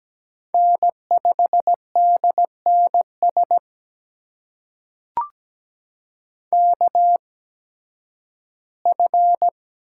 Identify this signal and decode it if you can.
Morse code, heard as a keyed tone, about 17 wpm, spaced as for 11 wpm, 700 Hz. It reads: N5DNS KF
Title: USA Call Signs - 8x Character Spacing - 17wpm